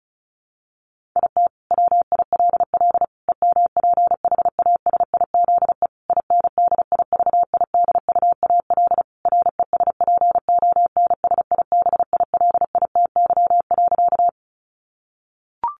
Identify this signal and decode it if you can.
Morse code, heard as a keyed tone, about 34 wpm, 700 Hz. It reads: IT WILL EMPHASIZE INDIVIDUAL RESPONSIBILITY.
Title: Sentences from Top 1500 Words 35wpm